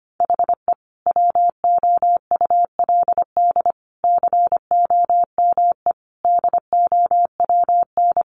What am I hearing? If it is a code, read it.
Morse code, heard as a keyed tone, about 25 wpm, 700 Hz. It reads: HE WOULD COME DOWN
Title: Sentences from Top 1300 Words 25wpm